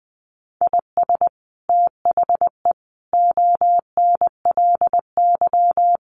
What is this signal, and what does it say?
Morse code, heard as a keyed tone, about 20 wpm, 700 Hz. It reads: IS THE ONLY
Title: Sets of 5 Words Encore 20wpm